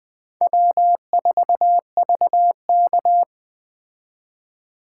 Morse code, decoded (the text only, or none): W4VK